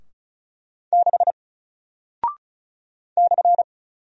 Morse code, heard as a keyed tone, about 36 wpm, 700 Hz. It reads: 6 /
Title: Single Character (Letter:Number:Punctuation:Prosign) Alt 35wpm